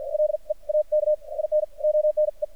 Morse code, heard as a keyed tone, about 52 wpm, spaced as for 41 wpm, 610 Hz. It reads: LEAK IN ONE
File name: archived / CRSnet